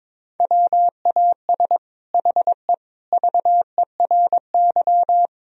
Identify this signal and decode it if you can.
Morse code, heard as a keyed tone, about 22 wpm, 700 Hz. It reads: WAS HE VERY
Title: Sentences from Top 1000 Words - Review Part 3 - 22wpm